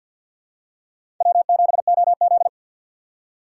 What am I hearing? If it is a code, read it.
Morse code, heard as a keyed tone, about 47 wpm, 700 Hz. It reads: W7OZ
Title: USA Call Signs - 4 Characters - 3x Character Spacing - 50wpm